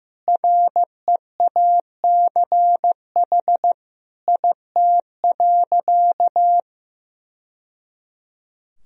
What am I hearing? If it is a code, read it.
Morse code, heard as a keyed tone, about 16 wpm, 700 Hz. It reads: REACH IT.
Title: Sentences from Top 1000 Words - Review Part 2 - 15wpm